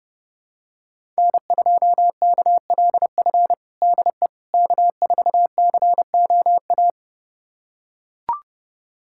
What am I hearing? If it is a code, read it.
Morse code, heard as a keyed tone, about 30 wpm, 700 Hz. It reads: N2KLF DE K4COA
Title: USA Callsign DE Callsign - 5 Characters - 30wpm